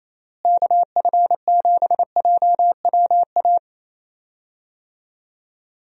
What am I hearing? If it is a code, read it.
Morse code, heard as a keyed tone, about 28 wpm, 700 Hz. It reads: KF7JWA
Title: USA Call Signs - 6 Characters - Repeated 2x - 28wpm